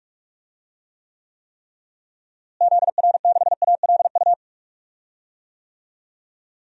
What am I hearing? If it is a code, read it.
Morse code, heard as a keyed tone, about 45 wpm, 700 Hz. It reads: ZR6ALU
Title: DXCC-100 Call Signs - 6 Characters - Repeated 3x - 45wpm